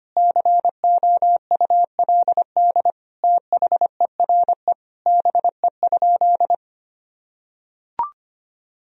Morse code, heard as a keyed tone, about 25 wpm, 700 Hz. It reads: COULD THERE BE?
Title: Sentences from Top 1000 Words - Review Part 3 - 25wpm